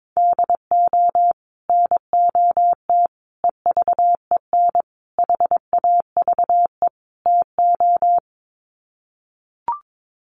Morse code, heard as a keyed tone, about 22 wpm, 700 Hz. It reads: DO NOT EVEN HAVE TO
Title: Sets of 5 Words Encore2 22wpm